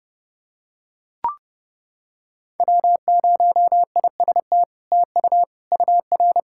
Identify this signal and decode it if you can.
Morse code, heard as a keyed tone, about 30 wpm, 700 Hz. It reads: W0IST TU UR
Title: SOTA QSO Practice 30wpm